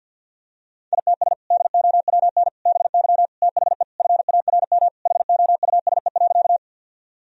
Morse code, heard as a keyed tone, about 46 wpm, 700 Hz. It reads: ITS DOWN BY THE FARM HOUSE.
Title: Sentences from Top 1000 Words - Review Part 2 - 50wpm